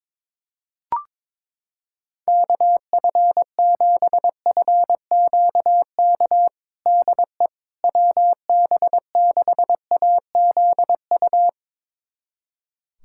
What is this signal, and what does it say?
Morse code, heard as a keyed tone, about 22 wpm, 700 Hz. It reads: KF7FQK DE WB6AZU
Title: USA Callsign DE Callsign - 6 Characters - 22wpm